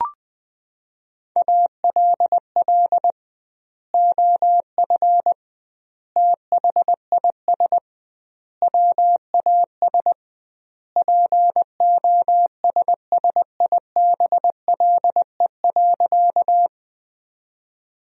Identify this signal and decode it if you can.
Morse code, heard as a keyed tone, about 20 wpm, 700 Hz. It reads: ALL OF THIS WAS POSSIBLE.
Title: 2x Word-Spacing - Sentences from Top 500 Words - 20wpm